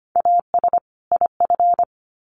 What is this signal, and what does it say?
Morse code, heard as a keyed tone, about 25 wpm, 700 Hz. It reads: AS IF
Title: Sets of 2 Words 25wpm